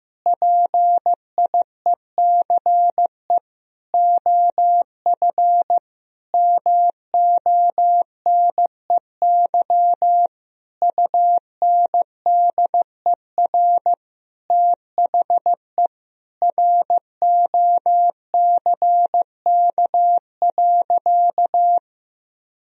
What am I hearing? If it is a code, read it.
Morse code, heard as a keyed tone, about 15 wpm, 700 Hz. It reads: PIECE OF MONEY UNDER THE ROCK.
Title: Sentences from Top 1000 Words - Review Part 2 - 15wpm